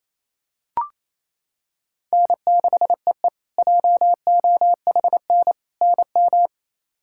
Morse code, heard as a keyed tone, about 27 wpm, 700 Hz. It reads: N6EE JOHN NM
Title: K1USN SST QSO Practice - 28wpm